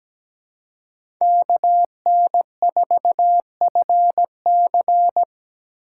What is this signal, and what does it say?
Morse code, heard as a keyed tone, about 17 wpm, 700 Hz. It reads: KN4FC
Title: USA Call Signs - 5 Characters - 6x Character Spacing - 17wpm